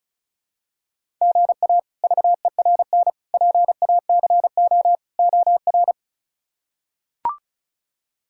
Morse code, heard as a keyed tone, about 34 wpm, 700 Hz. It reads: GA VERN PACO OR
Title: K1USN SST QSO Practice - 35wpm